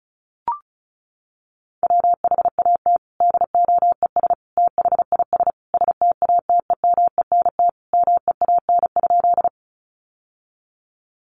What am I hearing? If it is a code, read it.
Morse code, heard as a keyed tone, about 34 wpm, 700 Hz. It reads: WHAT DOES THIS STATEMENT MEAN?
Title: Sentences from Top 1200 Words 35wpm